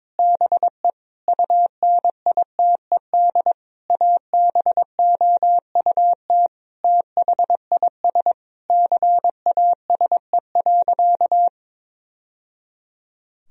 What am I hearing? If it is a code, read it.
Morse code, heard as a keyed tone, about 22 wpm, 700 Hz. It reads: BE UNITED ABOUT THIS CASE.
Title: Sentences from Top 1000 Words - Review Part 4 - 22wpm